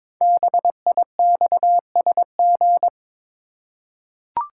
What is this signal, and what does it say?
Morse code, heard as a keyed tone, about 22 wpm, 700 Hz. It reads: BIXSG
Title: Code Groups - 5 Letters - 22wpm